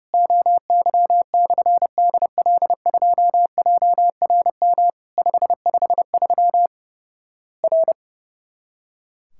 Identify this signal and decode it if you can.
Morse code, heard as a keyed tone, about 30 wpm, 700 Hz. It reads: OY/DL2JRM 553 R
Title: Worked All Europe DX Contest - QTC Exchange - 30wpm